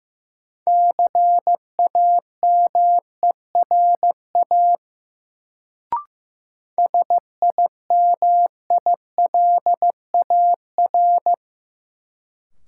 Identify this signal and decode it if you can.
Morse code, heard as a keyed tone, about 16 wpm, 700 Hz. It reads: CAMERA SIMILAR
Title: Sentences from Top 1000 Words - Review Part 5 - New Word Drill - 15wpm